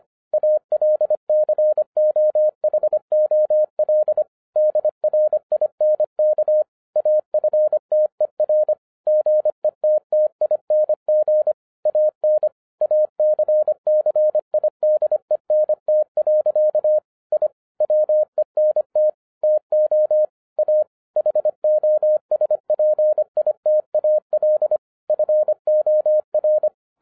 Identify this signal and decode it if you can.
Morse code, heard as a keyed tone, about 26 wpm, 610 Hz. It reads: ALCOHOL DRINK AFTER GETTING AN ACCIDENT. I WENT TO A HOSPITAL FOR